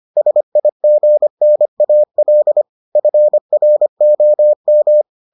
Morse code, heard as a keyed tone, about 25 wpm, 600 Hz. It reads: SIGNAL FROM